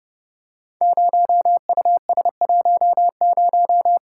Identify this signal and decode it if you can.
Morse code, heard as a keyed tone, about 30 wpm, 700 Hz. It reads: 0US10
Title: Code Groups - 5 Letter:Numbers - 30wpm